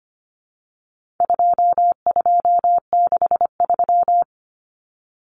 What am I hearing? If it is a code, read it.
Morse code, heard as a keyed tone, about 25 wpm, 700 Hz. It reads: 2263
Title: Common Cut Numbers - 4-Digit 25wpm